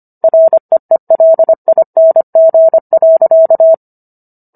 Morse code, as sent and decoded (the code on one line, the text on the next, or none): .-. . .-.. .. -. --. .-.-.-
REELING.